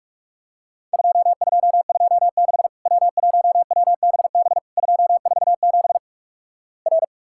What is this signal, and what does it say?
Morse code, heard as a keyed tone, about 46 wpm, 700 Hz. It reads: 2226 W1WBB 247 R
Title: Worked All Europe DX Contest - QTC Exchange - 45wpm